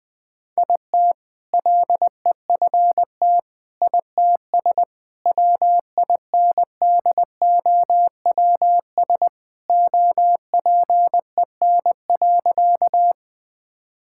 Morse code, heard as a keyed tone, about 20 wpm, 700 Hz. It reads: IT LEFT ITS WINDOWS OPEN.
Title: All-Age Appropriate Jokes - 20wpm